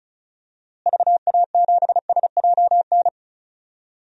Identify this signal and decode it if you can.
Morse code, heard as a keyed tone, about 34 wpm, 700 Hz. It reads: VA7SJN
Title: DXCC-100 Call Signs - 6 Characters - Repeated 3x - 35wpm